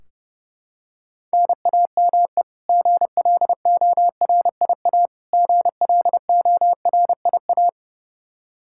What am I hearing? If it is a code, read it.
Morse code, heard as a keyed tone, about 32 wpm, 700 Hz. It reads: NAME GLORIA GLORIA
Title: K1USN SST QSO Practice 30wpm